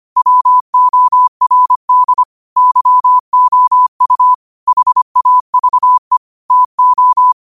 Morse code, heard as a keyed tone, about 25 wpm, 1000 Hz. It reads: WORD YOU HAVE TO
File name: qotdfiles / 25wpm